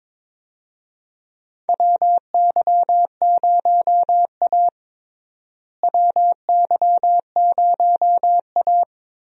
Morse code, heard as a keyed tone, about 22 wpm, 700 Hz. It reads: WY0A WY0A
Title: USA Call Signs - 4 Characters - Repeated 2x - 22wpm